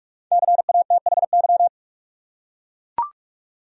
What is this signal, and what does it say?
Morse code, heard as a keyed tone, about 43 wpm, 700 Hz. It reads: CATHY
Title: USA Common Names - 45wpm